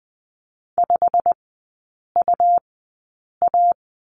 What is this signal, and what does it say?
Morse code, heard as a keyed tone, about 20 wpm, spaced as for 8 wpm, 700 Hz. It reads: 5UA
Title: 100 Lesson 17e - Letter u - Three Characters 20wpm